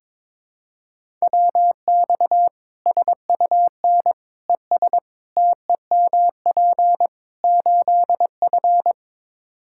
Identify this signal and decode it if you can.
Morse code, heard as a keyed tone, about 21 wpm, 700 Hz. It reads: WX SUN ES TEMP 8F
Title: Practice QSO - Exchange 2 - Info - 22wpm